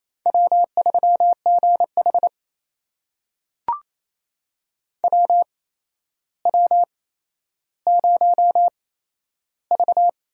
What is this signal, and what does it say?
Morse code, heard as a keyed tone, about 28 wpm, spaced as for 12 wpm, 700 Hz. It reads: W3GH WW0V
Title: USA Call Signs - 4 Characters - 8x Character Spacing - 28wpm